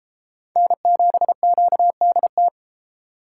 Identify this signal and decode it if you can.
Morse code, heard as a keyed tone, about 33 wpm, 700 Hz. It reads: N7QDT